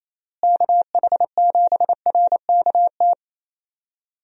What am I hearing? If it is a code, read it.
Morse code, heard as a keyed tone, about 28 wpm, 700 Hz. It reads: KH7RKT
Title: USA Call Signs - 6 Characters - Repeated 3x - 28wpm